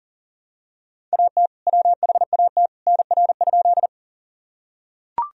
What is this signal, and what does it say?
Morse code, heard as a keyed tone, about 40 wpm, 700 Hz. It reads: AT WHAT NR?
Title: Sentences from Top 100 Words and Abbreviations 40wpm